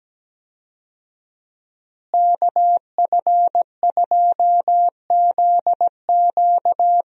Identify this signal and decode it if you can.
Morse code, heard as a keyed tone, about 17 wpm, 700 Hz. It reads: KF2ZQ
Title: USA Call Signs - 5 Characters - Repeated 3x - 17wpm